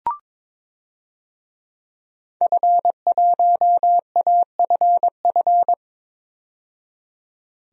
Morse code, heard as a keyed tone, about 21 wpm, 700 Hz. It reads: F1AFF
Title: DXCC-100 Call Signs - 5 Characters - 22wpm